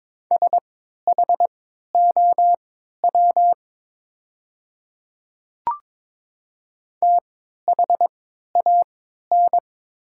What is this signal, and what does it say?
Morse code, heard as a keyed tone, about 22 wpm, spaced as for 12 wpm, 700 Hz. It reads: SHOW THAN
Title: 3x Character Spacing - Top 200 Words - 22wpm